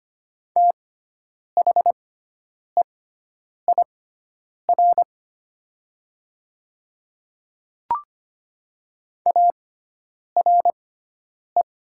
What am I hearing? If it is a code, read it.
Morse code, heard as a keyed tone, about 25 wpm, spaced as for 9 wpm, 700 Hz. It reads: THEIR ARE
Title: 6x Character Spacing - Top 100 Words - 25wpm